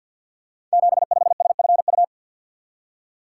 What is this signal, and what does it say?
Morse code, heard as a keyed tone, about 50 wpm, 700 Hz. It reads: 75SFU
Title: Code Groups - 5 Letter:Numbers - 50wpm